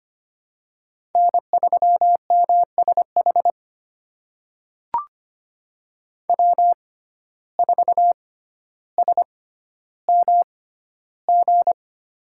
Morse code, heard as a keyed tone, about 25 wpm, spaced as for 13 wpm, 700 Hz. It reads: N3MSH W4SMG
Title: USA Call Signs - 5 Characters - 6x Character Spacing - 25wpm